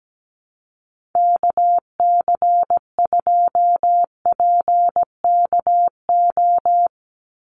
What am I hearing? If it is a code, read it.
Morse code, heard as a keyed tone, about 17 wpm, 700 Hz. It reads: KC2PKO
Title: USA Call Signs 17wpm